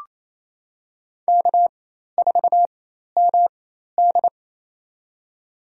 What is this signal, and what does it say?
Morse code, heard as a keyed tone, about 28 wpm, spaced as for 13 wpm, 700 Hz. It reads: K4MD
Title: USA Call Signs - 4 Characters - 4x Character Spacing - 28wpm